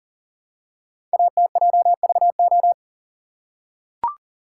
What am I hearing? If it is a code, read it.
Morse code, heard as a keyed tone, about 39 wpm, 690 Hz. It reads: ATJVO
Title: Code Groups - 5 Letters - 40wpm